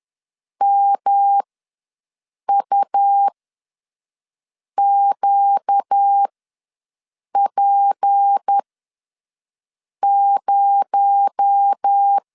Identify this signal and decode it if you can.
Morse code, heard as a keyed tone, about 10 wpm, spaced as for 5 wpm, 800 Hz. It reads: MUQP0
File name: LEZIONI / 6 PKX0 / 6 PKX0 ws 10wpm 4220